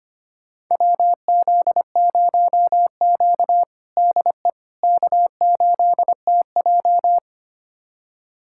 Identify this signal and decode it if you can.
Morse code, heard as a keyed tone, about 25 wpm, 700 Hz. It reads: WZ0Q DE K8TJ
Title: USA Callsign DE Callsign - 4 Characters - 25wpm